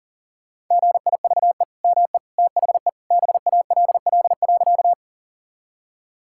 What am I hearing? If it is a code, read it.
Morse code, heard as a keyed tone, about 37 wpm, 690 Hz. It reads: GIVE ME THE BALL.
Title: Sentences from Top 1000 Words - Review Part 6 - 40wpm